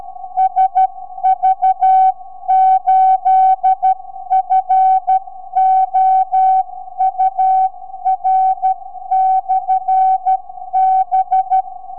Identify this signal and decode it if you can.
Morse code, heard as a keyed tone, about 12 wpm, 760 Hz. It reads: SV8FOUR/B
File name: sv8four 4m